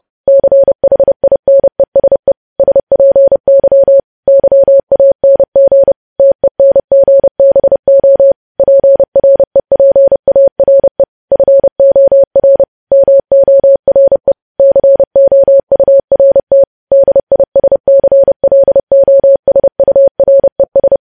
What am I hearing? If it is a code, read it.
Morse code, heard as a keyed tone, about 30 wpm, 550 Hz. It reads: CHINESE SPY YANG TENGBO PREPARE FOR MORE COURT DISCLOSURES
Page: News Headlines in Morse Code at 30 WPM